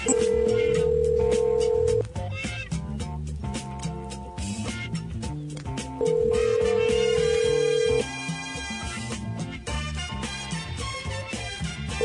Ringback tone (North America), 440 + 480 Hz.